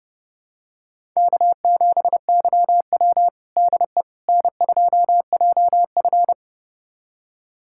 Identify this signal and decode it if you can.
Morse code, heard as a keyed tone, about 30 wpm, 700 Hz. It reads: K7YW DE N2JF
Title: USA Callsign DE Callsign - 4 Characters - 30wpm